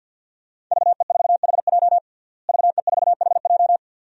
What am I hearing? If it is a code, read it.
Morse code, heard as a keyed tone, about 52 wpm, 700 Hz. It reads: VE4HJ VE4HJ
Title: DXCC-100 Call Signs - 5 Characters - Repeated 2x - 50wpm